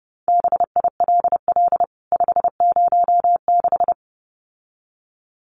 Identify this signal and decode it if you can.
Morse code, heard as a keyed tone, about 30 wpm, 700 Hz. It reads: BILL 506
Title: CWT Exchange 30wpm